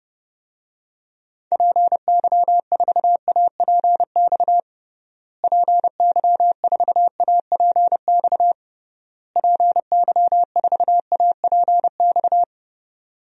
Morse code, heard as a keyed tone, about 30 wpm, 700 Hz. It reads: PY4APX PY4APX PY4APX
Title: DXCC-100 Call Signs - 6 Characters - Repeated 3x - 30wpm